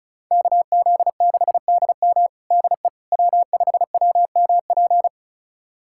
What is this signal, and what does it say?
Morse code, heard as a keyed tone, about 34 wpm, 700 Hz. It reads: KZ6DM DE W5WMP
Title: USA Callsign DE Callsign - 5 Characters - 35wpm